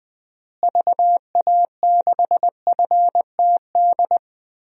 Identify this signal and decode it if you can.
Morse code, heard as a keyed tone, about 20 wpm, 700 Hz. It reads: VA6FTD
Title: DXCC-100 Call Signs - 6 Characters - Repeated 3x - 20wpm